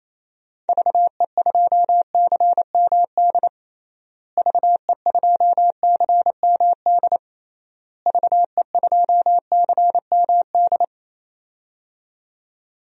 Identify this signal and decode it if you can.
Morse code, heard as a keyed tone, about 28 wpm, 700 Hz. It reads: VE2CMD VE2CMD VE2CMD
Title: DXCC-100 Call Signs - 6 Characters - Repeated 3x - 28wpm